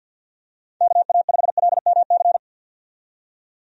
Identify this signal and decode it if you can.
Morse code, heard as a keyed tone, about 52 wpm, 700 Hz. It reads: KA5LMC
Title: USA Call Signs - 6 Characters - 50wpm